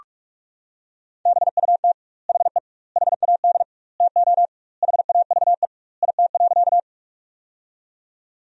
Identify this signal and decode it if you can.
Morse code, heard as a keyed tone, about 46 wpm, 700 Hz. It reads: BUT HE HAD TO HAVE IT.
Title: 2x Word-Spacing - Sentences from Top 100 Words - 45wpm